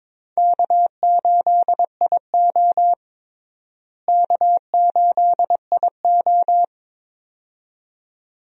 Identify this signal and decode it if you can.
Morse code, heard as a keyed tone, about 22 wpm, 700 Hz. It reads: K8IO K8IO
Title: USA Call Signs - 4 Characters - Repeated 2x - 22wpm